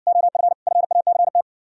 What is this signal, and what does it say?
Morse code, heard as a keyed tone, about 67 wpm, 700 Hz. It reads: OF FACT